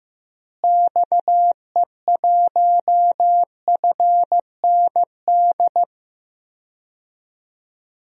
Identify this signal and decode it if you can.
Morse code, heard as a keyed tone, about 15 wpm, 700 Hz. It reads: XE1FND
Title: DXCC-100 Call Signs - 6 Characters - Repeated 3x - 15wpm